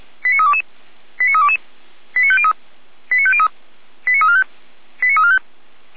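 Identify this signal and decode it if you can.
Five-tone selective call (ZVEI), heard as quick runs of five tones, tones of about 70 ms. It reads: 89322 89322 89503 89503 89345 89345
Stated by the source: Five-tone ZVEI selective call received on an SDR.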